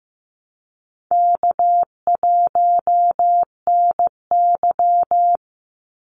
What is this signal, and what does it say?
Morse code, heard as a keyed tone, about 15 wpm, 700 Hz. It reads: K1NY
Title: Easy Call Signs - 15wpm